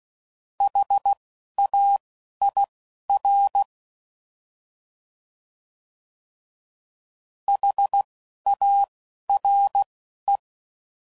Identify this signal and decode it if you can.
Morse code, heard as a keyed tone, about 16 wpm, spaced as for 11 wpm, 800 Hz. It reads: HAIR HARE